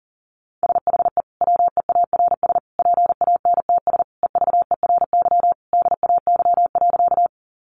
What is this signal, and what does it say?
Morse code, heard as a keyed tone, about 40 wpm, 700 Hz. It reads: SHE WEARS PANTS EVERY DAY.
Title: Additional Easy Phrases 40wpm